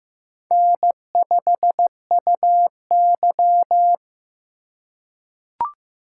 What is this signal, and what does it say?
Morse code, heard as a keyed tone, about 15 wpm, 700 Hz. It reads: N5UY
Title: USA Call Signs - 4 Characters - 3x Character Spacing - 15wpm